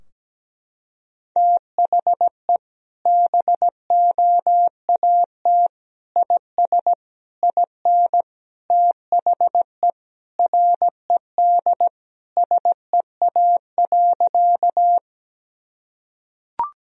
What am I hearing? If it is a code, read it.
Morse code, heard as a keyed tone, about 17 wpm, 700 Hz. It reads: THE BOAT IS IN THE RED SEA.
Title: Sentences from Top 1000 Words - Review Part 2 - 17wpm